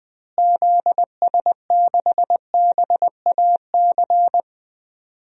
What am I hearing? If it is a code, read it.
Morse code, heard as a keyed tone, about 19 wpm, 700 Hz. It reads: ZS6BAC
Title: DXCC-100 Call Signs - 6 Characters - Repeated 3x - 20wpm